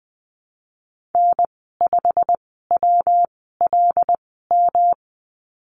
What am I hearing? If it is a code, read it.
Morse code, heard as a keyed tone, about 19 wpm, spaced as for 14 wpm, 700 Hz. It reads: N5WLM
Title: 238 Lesson 38g - Letter q - Call Signs 20wpm